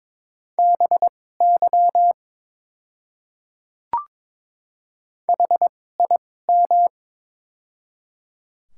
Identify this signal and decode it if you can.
Morse code, heard as a keyed tone, about 21 wpm, spaced as for 16 wpm, 700 Hz. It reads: BY HIM
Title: 2x Character Spacing - Top 100 Words - 22wpm